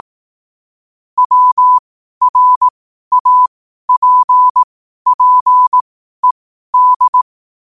Morse code, heard as a keyed tone, about 18 wpm, spaced as for 12 wpm, 1000 Hz. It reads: WRAPPED
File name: qotdfiles / 12wpm